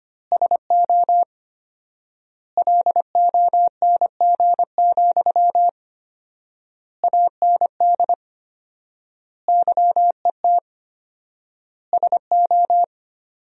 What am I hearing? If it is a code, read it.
Morse code, heard as a keyed tone, about 25 wpm, 700 Hz. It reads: SO LONG, AND YET SO
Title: 4x Word-Spacing - Sentences from Top 500 Words - 25wpm